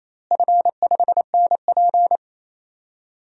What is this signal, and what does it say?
Morse code, heard as a keyed tone, about 28 wpm, 700 Hz. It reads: F5NP
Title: DXCC-100 Call Signs - 4 Characters - Repeated 3x - 28wpm